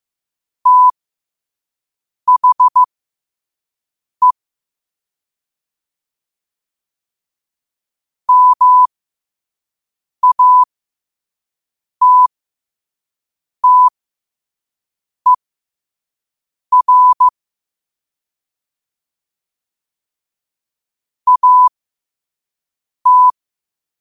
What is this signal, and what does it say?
Morse code, heard as a keyed tone, about 15 wpm, spaced as for 5 wpm, 1000 Hz. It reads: THE MATTER AT
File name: qotdfiles / 5wpm